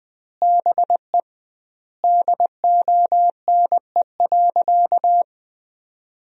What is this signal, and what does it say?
Morse code, heard as a keyed tone, about 21 wpm, 700 Hz. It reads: BE DONE.
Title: 2x Word-Spacing - Sentences from Top 500 Words - 20wpm